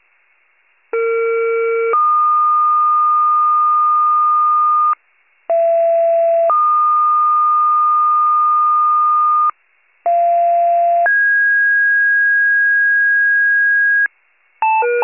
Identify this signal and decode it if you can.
Two-tone page, heard as a one-second tone, then a three-second tone, about 460 then 1220 Hz, about 670 then 1220 Hz, about 690 then 1650 Hz. It is heard from 0.9 to 14.1 s.